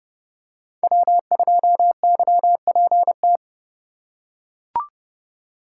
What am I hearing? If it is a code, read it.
Morse code, heard as a keyed tone, about 31 wpm, 700 Hz. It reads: W2YPT
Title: USA Call Signs - 5 Characters - 8x Character Spacing - 30wpm